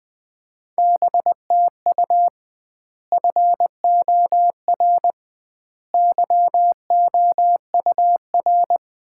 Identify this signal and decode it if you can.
Morse code, heard as a keyed tone, about 20 wpm, 700 Hz. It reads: BTU FOR YOUR
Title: Common QSO Phrases - 2x Word-Spacing - 20wpm